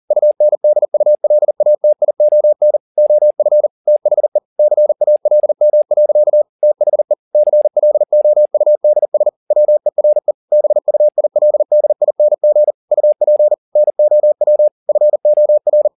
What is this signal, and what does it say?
Morse code, heard as a keyed tone, about 40 wpm, 600 Hz. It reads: UNDULATION OF THE CALM. THE CLOUDS WERE BUILDING UP NOW FOR